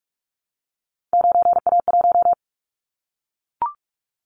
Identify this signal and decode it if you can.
Morse code, heard as a keyed tone, about 44 wpm, 700 Hz. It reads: 9A1
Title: Callsign Prefixes for Top 100 DXCC - 45wpm